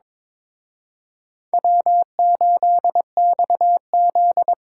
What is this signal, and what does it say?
Morse code, heard as a keyed tone, about 26 wpm, 700 Hz. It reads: W8XZ
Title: Speed Racing - USA 1x2 Call Signs 22wpm